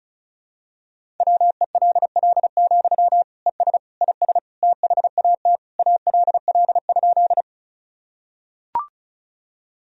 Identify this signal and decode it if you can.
Morse code, heard as a keyed tone, about 36 wpm, 700 Hz. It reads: WELL, ES IS THAT ALL?
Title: Sentences from Top 200 Words and Abbreviations 35wpm